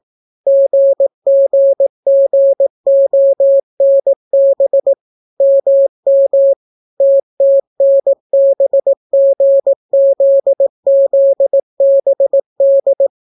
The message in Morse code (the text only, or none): GGGONB MM TTNBGZZBD